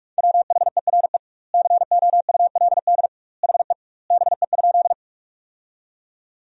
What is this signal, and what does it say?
Morse code, heard as a keyed tone, about 46 wpm, 700 Hz. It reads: WHERE COULD HE BE?
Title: 2x Word-Spacing - Sentences from Top 200 Words - 45wpm